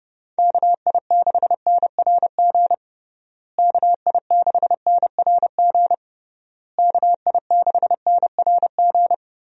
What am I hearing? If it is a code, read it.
Morse code, heard as a keyed tone, about 30 wpm, 700 Hz. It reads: KI6NRG KI6NRG KI6NRG
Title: USA Call Signs - 6 Characters - Repeated 3x - 30wpm